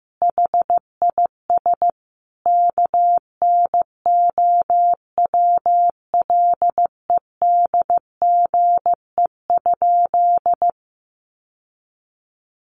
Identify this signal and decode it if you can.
Morse code, heard as a keyed tone, about 15 wpm, 700 Hz. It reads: HIS KNOWLEDGE?
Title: Sentences from Top 600 Words 15wpm